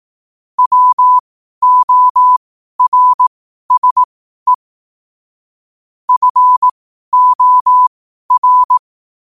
Morse code, heard as a keyed tone, about 18 wpm, spaced as for 12 wpm, 1000 Hz. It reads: WORSE FOR
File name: qotdfiles / 12wpm